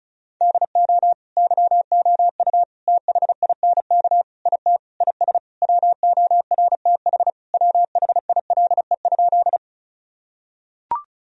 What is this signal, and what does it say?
Morse code, heard as a keyed tone, about 36 wpm, 700 Hz. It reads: DO YOU THINK IT IS WORTH WHILE?
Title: Sentences from Top 1000 Words - Review Part 1 - 35wpm